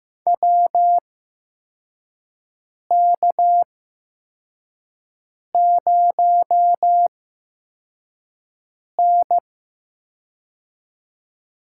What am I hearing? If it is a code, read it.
Morse code, heard as a keyed tone, about 15 wpm, spaced as for 4 wpm, 700 Hz. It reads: WK0N
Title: USA Call Signs - 4 Characters - 8x Character Spacing - 15wpm